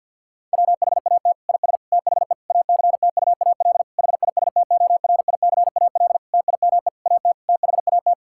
Morse code, heard as a keyed tone, about 50 wpm, 700 Hz. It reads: WHAT IS THE ACTUAL HISTORICAL TIME AT THAT
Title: Sentences from Top 1000 Words - Review Part 5 - 50wpm